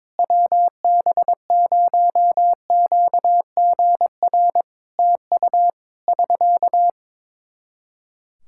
Morse code, heard as a keyed tone, about 22 wpm, 700 Hz. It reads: WB0QGR TU <SK>
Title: Practice QSO - Exchange 4 - Ending - 22wpm